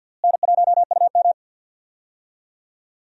Morse code, heard as a keyed tone, about 51 wpm, 700 Hz. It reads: N1UM
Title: USA Call Signs - 4 Characters - Repeated 2x - 50wpm